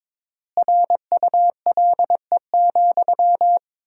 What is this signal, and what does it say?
Morse code, heard as a keyed tone, about 22 wpm, 700 Hz. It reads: RULE,
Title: 3x Word-Spacing - Sentences from Top 400 Words - 22wpm